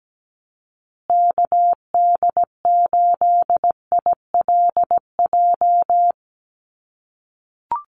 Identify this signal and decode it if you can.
Morse code, heard as a keyed tone, about 17 wpm, 700 Hz. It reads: KD8ILJ
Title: USA Call Signs Repeated 3x 17wpm